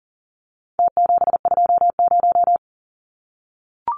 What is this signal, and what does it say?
Morse code, heard as a keyed tone, about 40 wpm, 700 Hz. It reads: T720
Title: Numbers - Contest Style - 4-Digit 40wpm